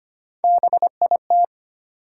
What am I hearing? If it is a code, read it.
Morse code, heard as a keyed tone, about 25 wpm, 700 Hz. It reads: BIT
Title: Sets of 3 Words - New Word Drill - 25wpm